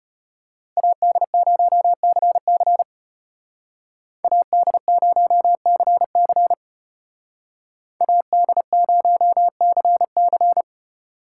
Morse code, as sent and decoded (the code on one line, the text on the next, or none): .- -.. ----- -.-. -.-. / .- -.. ----- -.-. -.-. / .- -.. ----- -.-. -.-.
AD0CC AD0CC AD0CC